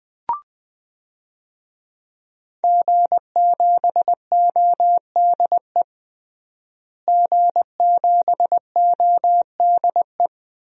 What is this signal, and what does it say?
Morse code, heard as a keyed tone, about 20 wpm, 700 Hz. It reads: G7ODE G7ODE
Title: DXCC-100 Call Signs - 5 Characters - Repeated 2x - 20wpm